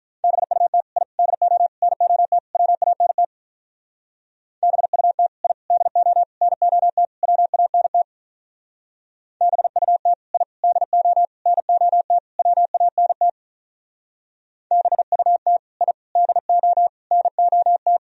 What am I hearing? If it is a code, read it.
Morse code, heard as a keyed tone, about 44 wpm, 700 Hz. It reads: BUT I DO NOT WANT BUT I DO NOT WANT BUT I DO NOT WANT BUT I DO NOT